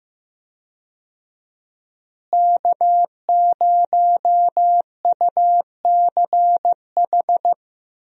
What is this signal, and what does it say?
Morse code, heard as a keyed tone, about 15 wpm, 700 Hz. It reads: K0UCH
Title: USA Call Signs - 5 Characters - Repeated 3x - 15wpm